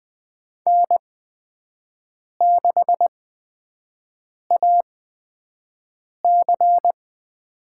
Morse code, heard as a keyed tone, about 20 wpm, spaced as for 5 wpm, 700 Hz. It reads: N6AC
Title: USA Call Signs - 4 Characters - 8x Character Spacing - 20wpm